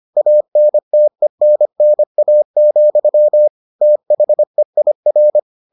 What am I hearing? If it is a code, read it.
Morse code, heard as a keyed tone, about 25 wpm, 600 Hz. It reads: ANTENNA, THEIR